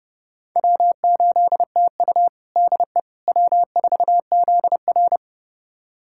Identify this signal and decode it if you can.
Morse code, heard as a keyed tone, about 32 wpm, 700 Hz. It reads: W8TU DE W4ZR
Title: USA Callsign DE Callsign - 4 Characters - 30wpm